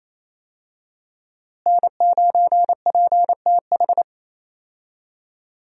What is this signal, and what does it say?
Morse code, heard as a keyed tone, about 28 wpm, 700 Hz. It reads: N9PTH
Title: USA Call Signs - 5 Characters - Repeated 2x - 28wpm